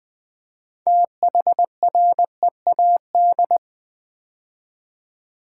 Morse code, heard as a keyed tone, about 20 wpm, 700 Hz. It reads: THREAD
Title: 6 Letter Words - First 3 Letters Given - 20wpm